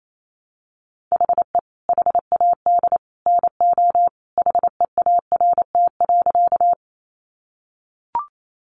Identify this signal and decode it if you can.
Morse code, heard as a keyed tone, about 28 wpm, 700 Hz. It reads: HE HAD NO HEART.
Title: Sentences from Top 1200 Words 28wpm